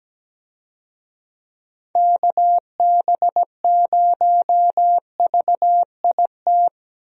Morse code, heard as a keyed tone, about 17 wpm, 700 Hz. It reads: KB0VIT
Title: USA Call Signs - 6 Characters - Repeated 3x - 17wpm